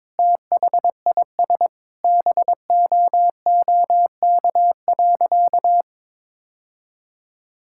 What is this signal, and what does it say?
Morse code, heard as a keyed tone, about 22 wpm, 700 Hz. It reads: THIS BOOK.
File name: practice / 22wpm / Sentences with 2-Word Phrases - 22wpm